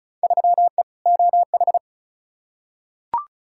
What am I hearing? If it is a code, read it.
Morse code, heard as a keyed tone, about 36 wpm, 700 Hz. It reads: 3E OH
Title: Field Day Exchanges - 35wpm